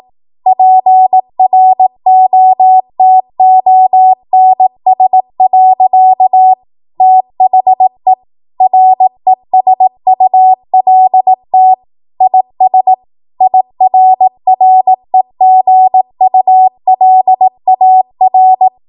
Morse code, heard as a keyed tone, about 18 wpm, 750 Hz. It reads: PROTONS. THE RESULT IS IRREGULAR
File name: Morse / 18WPM